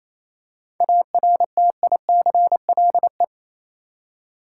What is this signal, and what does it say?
Morse code, heard as a keyed tone, about 27 wpm, 700 Hz. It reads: ARTICLE
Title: Seven Letter Words - 28wpm